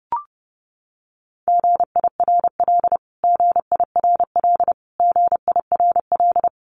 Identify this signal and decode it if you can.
Morse code, heard as a keyed tone, about 30 wpm, 700 Hz. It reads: GIRL GIRL GIRL
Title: Top 500 Words Repeated 30wpm